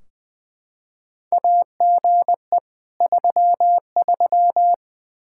Morse code, heard as a keyed tone, about 20 wpm, 700 Hz. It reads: AGE 33
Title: K1USN SST QSO Practice 20wpm